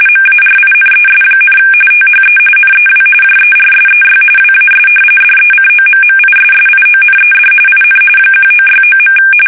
RTTY, about 45.45 baud. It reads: MARINES, THE NEW JERSEY WAS DECOMMISSIONED IN 1969.
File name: fsk850 / battleshipnj